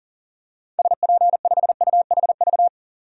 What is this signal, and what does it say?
Morse code, heard as a keyed tone, about 40 wpm, 700 Hz. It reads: SP5UHV
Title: DXCC-100 Call Signs - 6 Characters - 40wpm